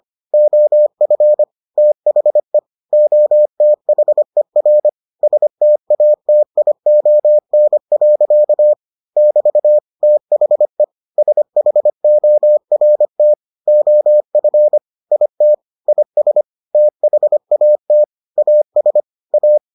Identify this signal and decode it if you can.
Morse code, heard as a keyed tone, about 25 wpm, 600 Hz. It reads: OF THE OTHER STATION. <BT> THE SHORT OF IT IS THAT AS A